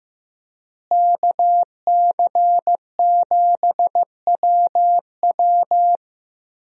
Morse code, heard as a keyed tone, about 16 wpm, 700 Hz. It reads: KC7WW
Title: USA Call Signs - 5 Characters - Repeated 3x - 15wpm